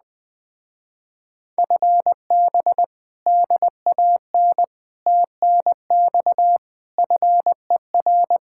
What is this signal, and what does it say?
Morse code, heard as a keyed tone, about 20 wpm, 700 Hz. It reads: FB DAN TNX FER
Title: Speed-Racing Common QSO Phrases - 17wpm